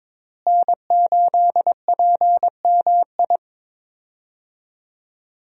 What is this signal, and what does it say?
Morse code, heard as a keyed tone, about 22 wpm, 700 Hz. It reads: N8PMI
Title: USA Call Signs - 5 Characters - Repeated 3x - 22wpm